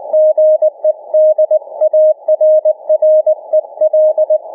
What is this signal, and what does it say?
Morse code, heard as a keyed tone, about 20 wpm, 590 Hz. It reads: GE DARREL